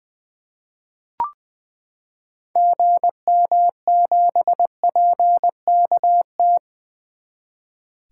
Morse code, heard as a keyed tone, about 21 wpm, 700 Hz. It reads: GM7PKT
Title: 2023 SOTA Activator Call Signs - 20wpm